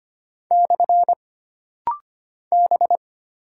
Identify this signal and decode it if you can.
Morse code, heard as a keyed tone, about 25 wpm, 700 Hz. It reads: / B
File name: Single Character (Letter:Number:Punctuation:Prosign) - Rapid-Fire - 25wpm